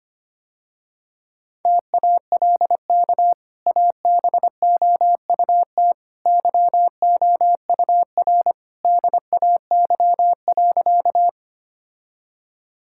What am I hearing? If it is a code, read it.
Morse code, heard as a keyed tone, about 25 wpm, 690 Hz. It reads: TALK ABOUT YOUR DAY.
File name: Easy Sentences with Hints - 25wpm